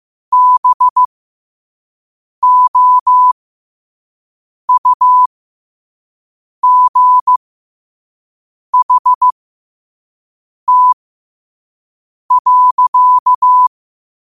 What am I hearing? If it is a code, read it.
Morse code, heard as a keyed tone, about 15 wpm, spaced as for 5 wpm, 1000 Hz. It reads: BOUGHT.
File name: qotdfiles / 5wpm